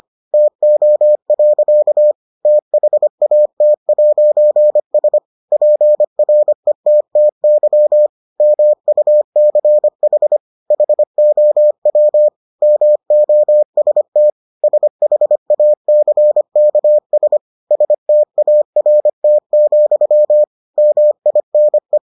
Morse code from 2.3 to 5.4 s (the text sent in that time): THAT'S